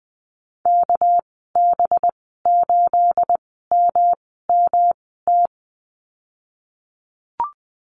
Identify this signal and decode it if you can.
Morse code, heard as a keyed tone, about 20 wpm, spaced as for 15 wpm, 700 Hz. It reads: KB8MMT
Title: 238 Lesson 38g - Letter q - Call Signs 20wpm